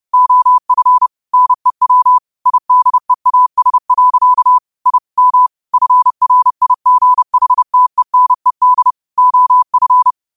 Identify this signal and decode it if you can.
Morse code, heard as a keyed tone, about 30 wpm, 1000 Hz. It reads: OF NEW IDEAS. I M FRIGHTENED OF
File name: qotdfiles / 30wpm